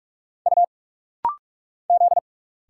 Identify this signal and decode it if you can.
Morse code, heard as a keyed tone, about 46 wpm, 690 Hz. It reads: U Z
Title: Single Letters - Rapid-Fire 45wpm